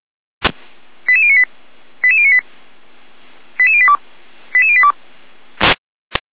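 Five-tone selective call (ZVEI), heard as quick runs of five tones, tones of about 80 ms. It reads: 88098 88098 88082 88082
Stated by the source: Five-tone ZVEI selective call received by SDR.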